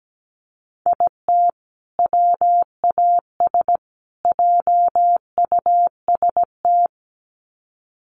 Morse code, heard as a keyed tone, about 17 wpm, 700 Hz. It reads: IT WAS JUST
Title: Sets of 3 Words 17wpm